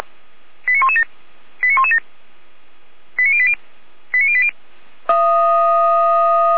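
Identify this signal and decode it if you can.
Five-tone selective call (ZVEI), heard as quick runs of five tones, tones of about 70 ms. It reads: 89117 89117 89080 89080
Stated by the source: ZVEI five-tone selective call received on SDR.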